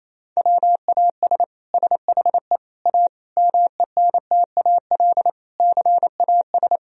Morse code, heard as a keyed tone, about 28 wpm, 700 Hz. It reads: WAS SHE A MENTAL CAS
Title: Sentences from Top 1000 Words - Review Part 4 - 28wpm